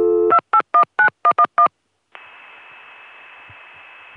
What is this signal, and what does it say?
DTMF, heard as telephone touch tones, tones of about 80 ms. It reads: 2819111